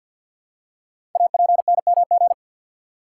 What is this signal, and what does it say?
Morse code, heard as a keyed tone, about 53 wpm, 700 Hz. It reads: APNMG
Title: Code Groups - 5 Letters - 50wpm